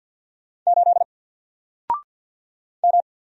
Morse code, heard as a keyed tone, about 50 wpm, 700 Hz. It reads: 8 M
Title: Single Letter-Number Alt 50wpm